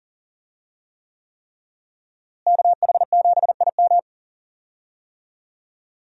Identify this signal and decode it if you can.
Morse code, heard as a keyed tone, about 40 wpm, 700 Hz. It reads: KH7IM
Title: USA Call Signs - 5 Characters - Repeated 3x - 40wpm